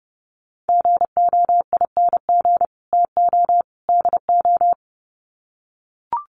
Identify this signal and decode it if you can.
Morse code, heard as a keyed tone, about 30 wpm, 700 Hz. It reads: GOING TO DO
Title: Sets of 3 Words 30wpm